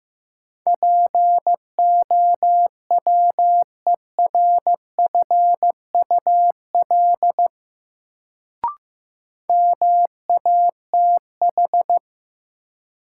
Morse code, heard as a keyed tone, about 15 wpm, 700 Hz. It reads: POWERFUL MATH
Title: Sentences from Top 1000 Words - Review Part 6 - New Word Drill - 15wpm